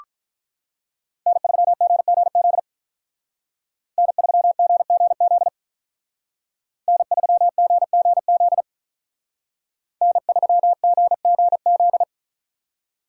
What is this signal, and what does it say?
Morse code, heard as a keyed tone, about 43 wpm, 700 Hz. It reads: N3GGZ N3GGZ N3GGZ N3GGZ
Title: Speed Racing - USA 1x3 Call Signs 35wpm